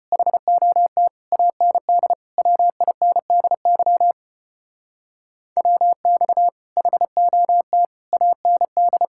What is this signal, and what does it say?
Morse code, heard as a keyed tone, about 32 wpm, 700 Hz. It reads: HOT AND WINDY WX HOT AND